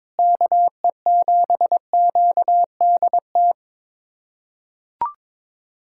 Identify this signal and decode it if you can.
Morse code, heard as a keyed tone, about 22 wpm, 700 Hz. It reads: KE7QDT
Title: USA Callsign DE Callsign - 6 Characters - 22wpm